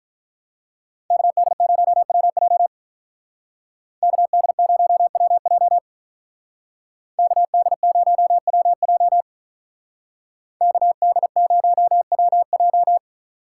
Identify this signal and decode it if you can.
Morse code, heard as a keyed tone, about 46 wpm, 700 Hz. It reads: KD0WJ KD0WJ KD0WJ KD0WJ